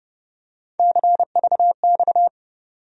Morse code, heard as a keyed tone, about 30 wpm, 700 Hz. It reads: CVX
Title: USA Call Sign Postfixes - 3 Letters 30wpm